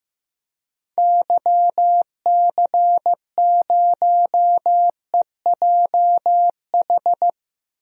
Morse code, heard as a keyed tone, about 15 wpm, 700 Hz. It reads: YC0EJH
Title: DXCC-100 Call Signs - 6 Characters - Repeated 3x - 15wpm